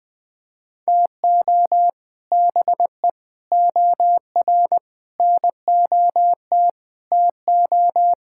Morse code, heard as a keyed tone, about 20 wpm, 700 Hz. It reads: TO BE OR NOT TO
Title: Sentences with 2-Word Phrases - 20wpm